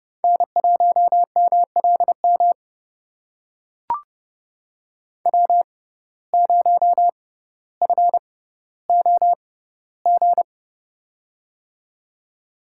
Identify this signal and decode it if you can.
Morse code, heard as a keyed tone, about 30 wpm, spaced as for 15 wpm, 700 Hz. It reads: N1MLM W0FOG
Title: USA Call Signs - 5 Characters - 6x Character Spacing - 30wpm